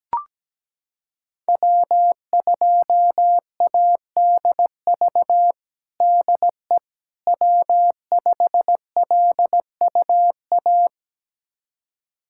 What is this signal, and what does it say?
Morse code, heard as a keyed tone, about 17 wpm, 700 Hz. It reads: W2ADV DE W5LUA
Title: USA Callsign DE Callsign - 5 Characters - 17wpm